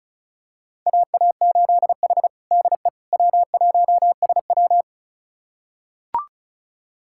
Morse code, heard as a keyed tone, about 34 wpm, 700 Hz. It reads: AA8H DE W1SW
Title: USA Callsign DE Callsign - 4 Characters - 35wpm